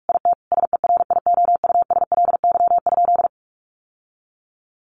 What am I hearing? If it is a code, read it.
Morse code, heard as a keyed tone, about 44 wpm, 700 Hz. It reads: IT SERIOUSLY?
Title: Sentences from Top 1200 Words 45wpm